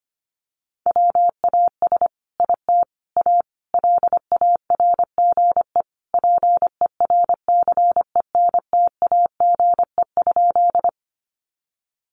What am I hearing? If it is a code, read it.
Morse code, heard as a keyed tone, about 25 wpm, 700 Hz. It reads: WAS IT A LARGE PERCENTAGE?
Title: Sentences from Top 1300 Words 25wpm